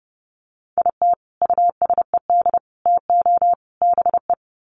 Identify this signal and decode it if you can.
Morse code, heard as a keyed tone, about 28 wpm, 700 Hz. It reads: IT USED TO BE
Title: Sets of 4 Words 30wpm